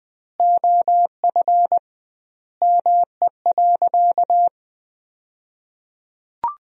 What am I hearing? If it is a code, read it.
Morse code, heard as a keyed tone, about 20 wpm, 700 Hz. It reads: OF ME.
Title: 2x Word-Spacing - Sentences from Top 300 Words - 20wpm